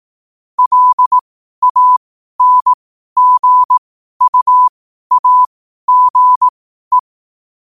Morse code, heard as a keyed tone, about 18 wpm, spaced as for 12 wpm, 1000 Hz. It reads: LANGUAGE
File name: qotdfiles / 12wpm